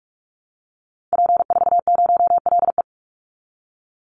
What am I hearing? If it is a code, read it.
Morse code, heard as a keyed tone, about 44 wpm, 700 Hz. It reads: P40LE
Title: Call Signs Repeated 2x - 45wpm